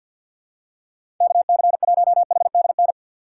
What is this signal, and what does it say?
Morse code, heard as a keyed tone, about 52 wpm, 700 Hz. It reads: KC1HDN
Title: USA Call Signs - 6 Characters - 6x Character Spacing - 50wpm